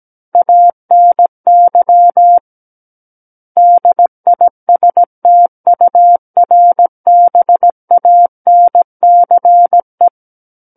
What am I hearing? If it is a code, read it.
Morse code, heard as a keyed tone, about 17 wpm, 700 Hz. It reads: ANY DISTURBANCE
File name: Book One / 08-Chapter 8 - 17 WPM